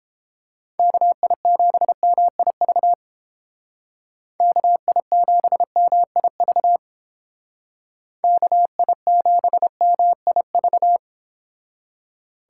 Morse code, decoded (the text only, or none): KI7MIV KI7MIV KI7MIV